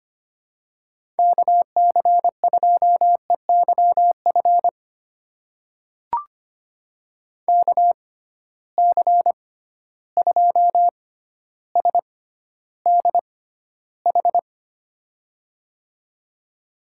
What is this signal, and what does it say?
Morse code, heard as a keyed tone, about 25 wpm, spaced as for 13 wpm, 700 Hz. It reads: KC2EYF KC2SDH
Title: USA Call Signs - 6 Characters - 6x Character Spacing - 25wpm